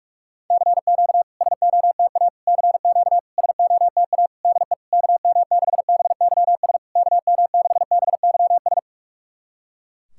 Morse code, heard as a keyed tone, about 48 wpm, 700 Hz. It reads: CQ SOTA CQ SOTA DE KM6BYS KM6BYS
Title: SOTA QSO Practice 45wpm